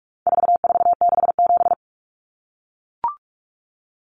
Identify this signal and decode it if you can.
Morse code, heard as a keyed tone, about 44 wpm, 700 Hz. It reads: 4467
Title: Numbers - 4-Digit 45wpm